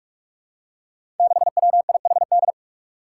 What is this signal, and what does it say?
Morse code, heard as a keyed tone, about 41 wpm, 700 Hz. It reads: 6WIHD
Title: Five Challenging Letter-Number Combinations 45wpm